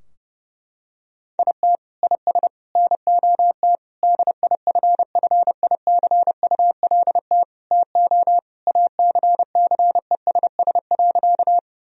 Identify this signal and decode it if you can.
Morse code, heard as a keyed tone, about 30 wpm, 700 Hz. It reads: IT IS NOT DIFFICULT TO ACCESS.
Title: Sentences from Top 1000 Words - Review Part 4 - 30wpm